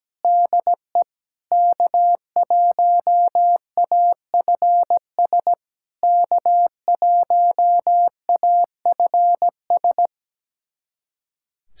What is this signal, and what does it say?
Morse code, heard as a keyed tone, about 17 wpm, 700 Hz. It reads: DE K1AFS K1AFS
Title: SOTA QSO Practice 17wpm